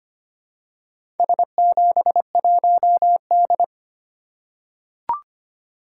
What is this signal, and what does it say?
Morse code, heard as a keyed tone, about 25 wpm, 700 Hz. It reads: S71D
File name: practice / 25wpm / Four Letter-Number Combinations - 25wpm